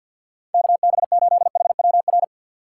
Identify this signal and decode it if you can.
Morse code, heard as a keyed tone, about 48 wpm, 700 Hz. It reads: KB8HWR
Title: USA Call Signs - 6 Characters - 3x Character Spacing - 50wpm